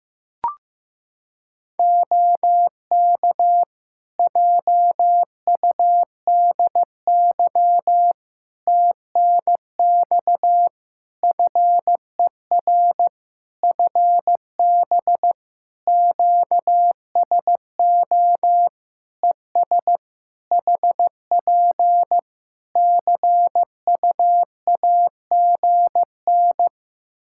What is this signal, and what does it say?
Morse code, heard as a keyed tone, about 16 wpm, 700 Hz. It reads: OK JUDY TNX FER FB QSO ES HP CUAGN
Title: Practice QSO - Exchange 4 - Ending - 15wpm